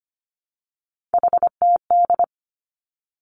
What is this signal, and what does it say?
Morse code, heard as a keyed tone, about 25 wpm, 700 Hz. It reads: HTD
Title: Three Challenging Letter-Number Combinations 25wpm